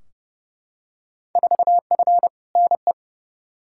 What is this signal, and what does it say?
Morse code, heard as a keyed tone, about 29 wpm, 700 Hz. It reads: 4F NE
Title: Field Day Exchanges Encore - 30wpm